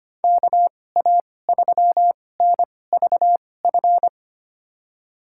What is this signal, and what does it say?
Morse code, heard as a keyed tone, about 25 wpm, spaced as for 18 wpm, 700 Hz. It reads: KA3NVF
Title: USA Call Signs - 6 Characters - 2x Character Spacing - 25wpm